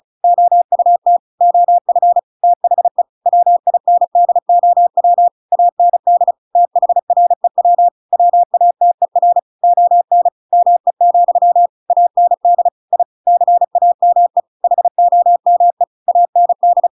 Morse code, heard as a keyed tone, about 34 wpm, 700 Hz. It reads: OUT OF THE WINDOW AND THREW WATER ON ME, AND I CAME HOME AND
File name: Pinocchio / Pinocchio ch7 35 35 ik1hge